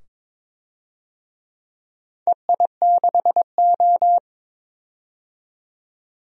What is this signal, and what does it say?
Morse code, heard as a keyed tone, about 22 wpm, 700 Hz. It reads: EI6O
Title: DXCC-100 Call Signs - 4 Characters - 22wpm